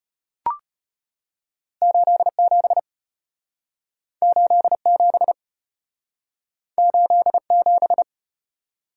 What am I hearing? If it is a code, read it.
Morse code, heard as a keyed tone, about 33 wpm, 700 Hz. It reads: 87 87 87